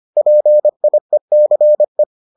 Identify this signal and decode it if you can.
Morse code, heard as a keyed tone, about 25 wpm, 600 Hz. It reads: PIECE